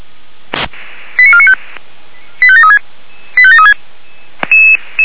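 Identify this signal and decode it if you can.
Five-tone selective call (ZVEI), heard as quick runs of five tones, tones of about 70 ms. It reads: 89495 85736 85736
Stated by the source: ZVEI five-tone selective call received by SDR.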